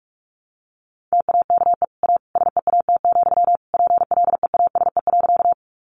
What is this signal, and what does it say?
Morse code, heard as a keyed tone, about 44 wpm, 700 Hz. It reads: TAKE A SEAT, PLEASE.
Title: Sentences from Top 1500 Words 45wpm